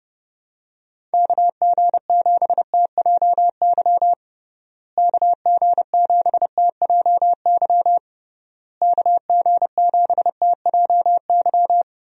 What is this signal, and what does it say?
Morse code, heard as a keyed tone, about 30 wpm, 700 Hz. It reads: KG7TJY KG7TJY KG7TJY
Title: USA Call Signs - 6 Characters - Repeated 3x - 30wpm